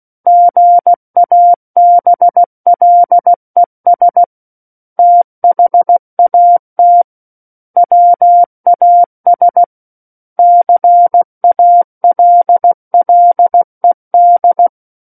Morse code, heard as a keyed tone, about 16 wpm, 700 Hz. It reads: GABLES THAT WAS CALLED
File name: Book One / 07-Chapter 7 - 16 WPM